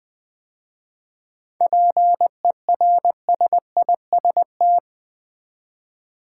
Morse code, heard as a keyed tone, about 20 wpm, 700 Hz. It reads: PERSIST
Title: 7 Letter Words - First 3 Letters Given - 20wpm